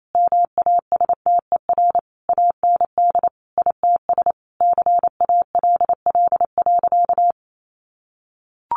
Morse code, heard as a keyed tone, about 28 wpm, 700 Hz. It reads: MASTER AND ITS CALL.
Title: Sentences from Top 1300 Words 28wpm